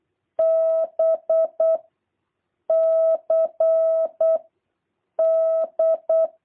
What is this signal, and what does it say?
Morse code, heard as a keyed tone, about 8 wpm, 650 Hz. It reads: B C D
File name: jh8cbh / 2023cw